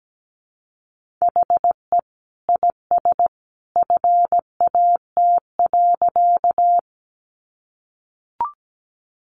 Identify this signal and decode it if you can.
Morse code, heard as a keyed tone, about 17 wpm, 700 Hz. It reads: HE IS FAT.
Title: Sentences from Top 600 Words 17wpm